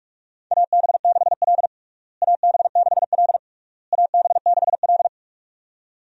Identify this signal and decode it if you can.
Morse code, heard as a keyed tone, about 46 wpm, 700 Hz. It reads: AB6L AB6L AB6L
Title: USA Call Signs - 4 Characters - Repeated 3x - 45wpm